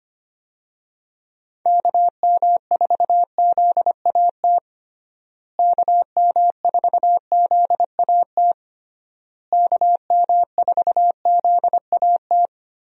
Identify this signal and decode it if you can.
Morse code, heard as a keyed tone, about 25 wpm, 700 Hz. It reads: KM4ZAT KM4ZAT KM4ZAT
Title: USA Call Signs - 6 Characters - Repeated 3x - 25wpm